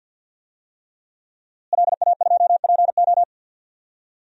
Morse code, heard as a keyed tone, about 49 wpm, 700 Hz. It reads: LA2PO
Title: DXCC-100 Call Signs - 5 Characters - Repeated 3x - 50wpm